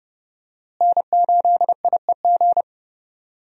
Morse code, heard as a keyed tone, about 30 wpm, 700 Hz. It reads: N8IEG
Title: USA Call Signs - 5 Characters - Repeated 2x - 30wpm